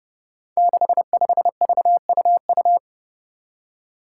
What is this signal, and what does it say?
Morse code, heard as a keyed tone, about 30 wpm, 700 Hz. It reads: 65VUU
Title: Five Challenging Letter-Number Combinations 30wpm